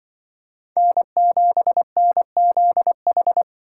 Morse code, heard as a keyed tone, about 24 wpm, 700 Hz. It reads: N7NZH
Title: Speed Racing - USA Call Signs 20wpm